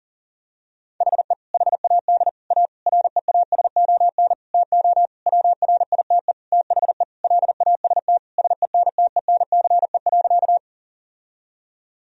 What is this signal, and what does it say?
Morse code, heard as a keyed tone, about 40 wpm, 700 Hz. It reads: HE HAD A REASON TO WRITE THE LAST SENTENCE.
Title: Sentences from Top 1000 Words - Review Part 1 - 40wpm